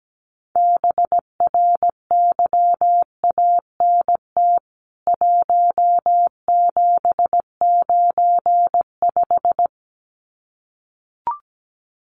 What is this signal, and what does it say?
Morse code, heard as a keyed tone, about 17 wpm, 700 Hz. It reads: BRYANT 1795
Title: CWT Exchange 17wpm